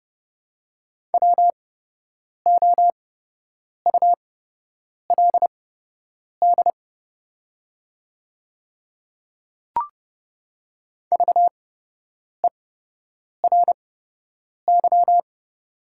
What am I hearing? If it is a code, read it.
Morse code, heard as a keyed tone, about 30 wpm, spaced as for 8 wpm, 700 Hz. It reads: WOULD VERY
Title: 8x Character Spacing - Top 300 Words - 30wpm